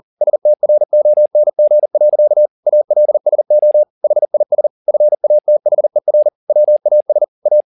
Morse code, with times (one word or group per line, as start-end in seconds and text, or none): 0.2-2.5 STRONG.
2.7-3.8 ALSO
4.0-4.7 HIS
4.9-6.3 FATHER
6.5-7.2 WAS
7.5-7.6 A